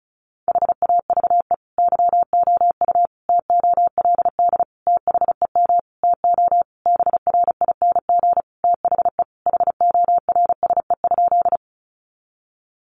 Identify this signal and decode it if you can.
Morse code, heard as a keyed tone, about 34 wpm, 700 Hz. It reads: HAVE YOU TOLD THEM TO BRING THE HORSE?
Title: Sentences from Top 500 Words 35wpm